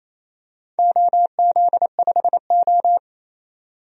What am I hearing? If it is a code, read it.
Morse code, heard as a keyed tone, about 28 wpm, 700 Hz. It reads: OZ5O
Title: DXCC-100 Call Signs - 4 Characters - 28wpm